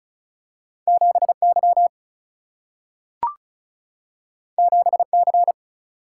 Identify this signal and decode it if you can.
Morse code, heard as a keyed tone, about 36 wpm, 700 Hz. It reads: 7Y 7C
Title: Single-Letter Single-Number 35wpm